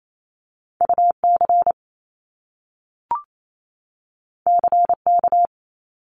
Morse code, heard as a keyed tone, about 28 wpm, 700 Hz. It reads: UC CK
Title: Top Two Letter Combinations in Words 28wpm